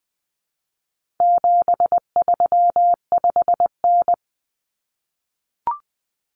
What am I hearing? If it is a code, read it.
Morse code, heard as a keyed tone, about 20 wpm, 700 Hz. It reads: 735N
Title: Common Cut Numbers - 4-Digit 20wpm